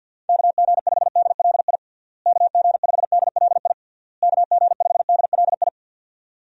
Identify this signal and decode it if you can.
Morse code, heard as a keyed tone, about 52 wpm, 700 Hz. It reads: KG5DLI KG5DLI KG5DLI
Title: USA Call Signs - 6 Characters - Repeated 3x - 50wpm